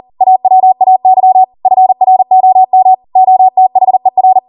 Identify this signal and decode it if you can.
Morse code, heard as a keyed tone, about 40 wpm, 750 Hz. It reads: AWAY FROM OTHER